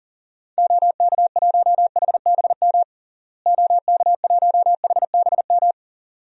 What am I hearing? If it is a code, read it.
Morse code, heard as a keyed tone, about 40 wpm, 700 Hz. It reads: OK1HBM OK1HBM
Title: DXCC-100 Call Signs - 6 Characters - Repeated 2x - 40wpm